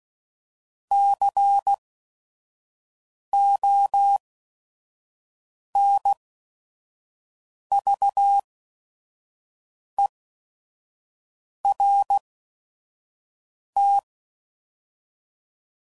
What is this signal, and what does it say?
Morse code, heard as a keyed tone, about 16 wpm, spaced as for 5 wpm, 800 Hz. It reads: CONVERT
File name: CW Code Groups @5WPM